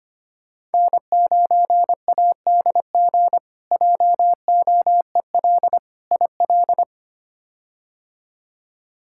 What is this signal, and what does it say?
Morse code, heard as a keyed tone, about 25 wpm, 700 Hz. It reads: N9ADG JOEL IL
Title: K1USN SST QSO Practice - 25wpm